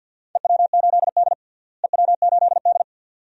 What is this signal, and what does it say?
Morse code, heard as a keyed tone, about 50 wpm, 700 Hz. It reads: EW8D EW8D
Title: DXCC-100 Call Signs - 4 Characters - Repeated 2x - 50wpm